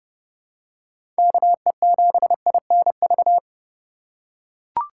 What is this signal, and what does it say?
Morse code, heard as a keyed tone, about 30 wpm, 700 Hz. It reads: KE7INV